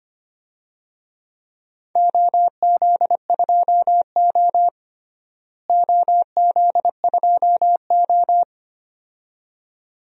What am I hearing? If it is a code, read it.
Morse code, heard as a keyed tone, about 25 wpm, 700 Hz. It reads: OZ2O OZ2O
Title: DXCC-100 Call Signs - 4 Characters - Repeated 2x - 25wpm